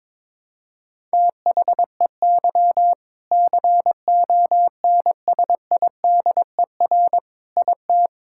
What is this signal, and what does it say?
Morse code, heard as a keyed tone, about 22 wpm, 700 Hz. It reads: THEY CONSIDER IT
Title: Sentences from Top 1000 Words - Review Part 5 - 22wpm